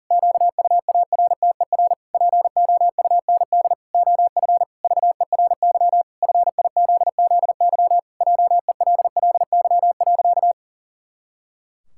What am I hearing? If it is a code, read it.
Morse code, heard as a keyed tone, about 40 wpm, 700 Hz. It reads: QUARTER POUND OF VERY FIZZY JELLY.
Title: Pangram Challenge v2 - 40wpm